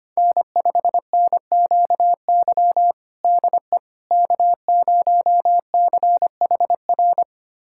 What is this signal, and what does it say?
Morse code, heard as a keyed tone, about 25 wpm, 700 Hz. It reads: N5NQY DE K0CHR
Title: USA Callsign DE Callsign - 5 Characters - 25wpm